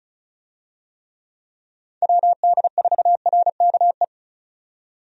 Morse code, heard as a keyed tone, about 36 wpm, 700 Hz. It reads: WD4RKE
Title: USA Call Signs - 6 Characters - 35wpm